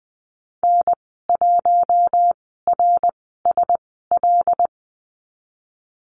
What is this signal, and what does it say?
Morse code, heard as a keyed tone, about 19 wpm, spaced as for 14 wpm, 700 Hz. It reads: N1RSL
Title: 082 Lesson 14g - Number 2 - Call Signs 20wpm